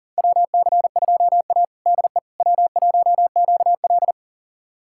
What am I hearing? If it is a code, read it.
Morse code, heard as a keyed tone, about 40 wpm, 700 Hz. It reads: WC2A DE W1QL
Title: USA Callsign DE Callsign - 4 Characters - 40wpm